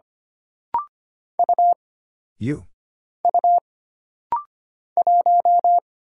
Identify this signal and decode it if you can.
Morse code, heard as a keyed tone, about 25 wpm, 700 Hz. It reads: U U 1
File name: Single Character (Letter:Number:Punctuation:Prosign) - Rapid-Fire - 25wpm